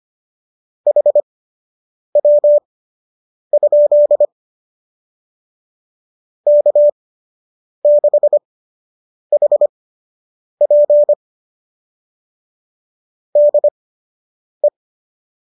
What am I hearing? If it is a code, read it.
Morse code, heard as a keyed tone, about 25 wpm, spaced as for 8 wpm, 590 Hz. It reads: HW? K6HP DE
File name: qso003 08